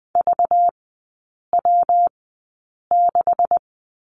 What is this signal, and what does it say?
Morse code, heard as a keyed tone, about 20 wpm, spaced as for 8 wpm, 700 Hz. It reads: VW6
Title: 198 Lesson 32e - Letter k - Three Characters 20wpm